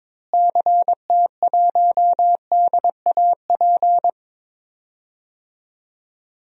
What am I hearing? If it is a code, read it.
Morse code, heard as a keyed tone, about 22 wpm, 700 Hz. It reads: CT1DAP
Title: DXCC-100 Call Signs - 6 Characters - Repeated 3x - 22wpm